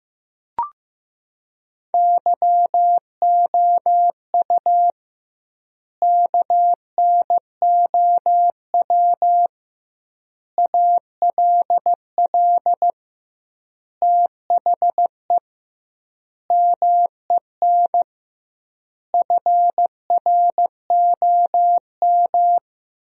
Morse code, heard as a keyed tone, about 15 wpm, 700 Hz. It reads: YOU KNOW ALL THE MEN FROM
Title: 2x Word-Spacing - Sentences from Top 300 Words - 15wpm